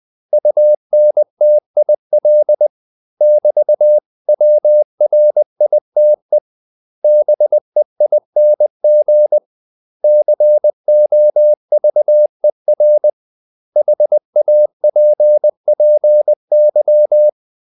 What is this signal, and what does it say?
Morse code, heard as a keyed tone, about 20 wpm, 600 Hz. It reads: UNTIL <BT>WRITE BEING COVER HAPPY
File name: CW / Practice / 500MostCommonEnglishWords20